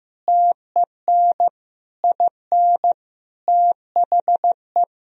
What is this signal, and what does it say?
Morse code, heard as a keyed tone, about 16 wpm, 700 Hz. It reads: TEN IN THE
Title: Sentences from Top 1000 Words - Review Part 2 - 15wpm